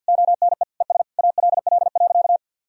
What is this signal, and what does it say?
Morse code, heard as a keyed tone, about 52 wpm, 700 Hz. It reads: ONE ES ALL.
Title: Sentences from Top 200 Words and Abbreviations 50wpm